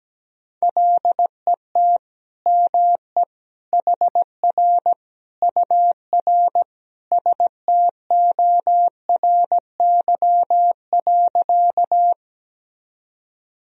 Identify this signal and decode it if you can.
Morse code, heard as a keyed tone, about 17 wpm, 700 Hz. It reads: LET ME HR UR STORY.
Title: Sentences from Top 300 Words and Abbreviations 17wpm